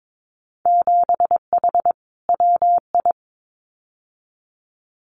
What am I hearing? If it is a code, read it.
Morse code, heard as a keyed tone, about 22 wpm, 700 Hz. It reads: 7H WI
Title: Winter Field Day Exchanges Encore 22wpm